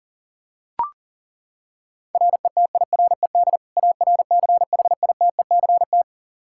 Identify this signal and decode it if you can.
Morse code, heard as a keyed tone, about 40 wpm, 700 Hz. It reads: RETIRED ARCHITECT
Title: Practice QSO - Exchange 3 - More Info - 40wpm